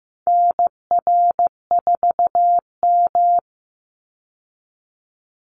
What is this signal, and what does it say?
Morse code, heard as a keyed tone, about 15 wpm, 700 Hz. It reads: NR4M
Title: Call Signs Repeated 2x - 15wpm